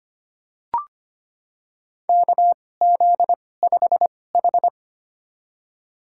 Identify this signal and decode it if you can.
Morse code, heard as a keyed tone, about 25 wpm, spaced as for 18 wpm, 700 Hz. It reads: KZ5H
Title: USA Call Signs - 4 Characters - 2x Character Spacing - 25wpm